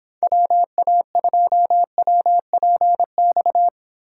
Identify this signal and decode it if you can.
Morse code, heard as a keyed tone, about 29 wpm, 700 Hz. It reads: WA2WPX